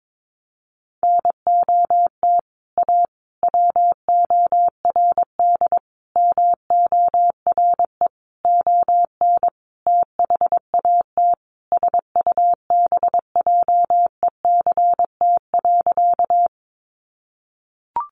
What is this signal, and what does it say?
Morse code, heard as a keyed tone, about 22 wpm, 700 Hz. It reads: NOT A WORD MORE ON THAT SUBJECT.
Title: Sentences from Top 800 Words 22wpm